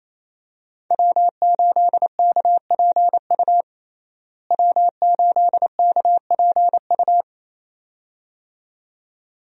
Morse code, heard as a keyed tone, about 27 wpm, 700 Hz. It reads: W8KPU W8KPU
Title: USA Call Signs - 5 Characters - Repeated 2x - 28wpm